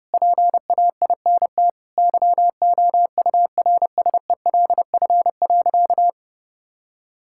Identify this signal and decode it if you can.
Morse code, heard as a keyed tone, about 30 wpm, 700 Hz. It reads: PAINT YOURSELF.
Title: Sentences from Top 1000 Words - Review Part 6 - 30wpm